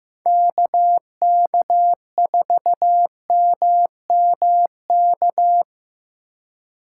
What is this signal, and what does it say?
Morse code, heard as a keyed tone, about 15 wpm, 700 Hz. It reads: KK4MMK
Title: USA Call Signs - 6 Characters - 3x Character Spacing - 15wpm